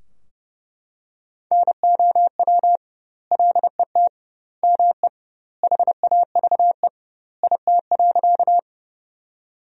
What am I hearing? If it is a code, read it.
Morse code, heard as a keyed tone, about 30 wpm, 700 Hz. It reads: NOW LET ME HAVE IT.
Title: 2x Word-Spacing - Sentences from Top 300 Words - 30wpm